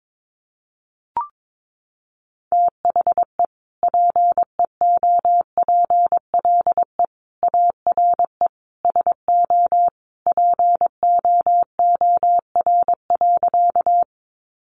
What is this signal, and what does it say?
Morse code, heard as a keyed tone, about 22 wpm, 700 Hz. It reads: THE PEOPLE ARE SO POOR.
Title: Sentences from Top 1000 Words 22wpm